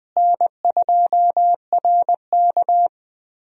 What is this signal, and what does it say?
Morse code, heard as a keyed tone, about 20 wpm, 700 Hz. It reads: N2RK
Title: USA Call Signs - 4 Characters - Repeated 2x - 20wpm